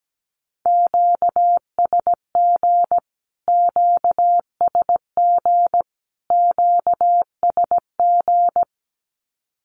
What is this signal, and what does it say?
Morse code, heard as a keyed tone, about 17 wpm, 700 Hz. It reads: QSG QSG QSG
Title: Q-Codes Repeated 3x 17wpm